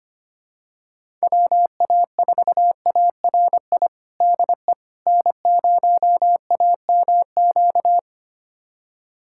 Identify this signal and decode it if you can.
Morse code, heard as a keyed tone, about 25 wpm, 700 Hz. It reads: WA4ARI DE N0AMQ
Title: USA Callsign DE Callsign 25wpm